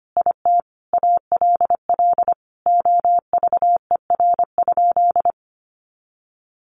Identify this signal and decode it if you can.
Morse code, heard as a keyed tone, about 25 wpm, 700 Hz. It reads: IT ALL OVER?
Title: Sentences from Top 1200 Words 25wpm